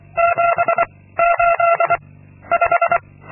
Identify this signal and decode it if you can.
Morse code, heard as a keyed tone, about 23 wpm, 680 Hz. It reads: 7 8 5
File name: Linksmakalnis-2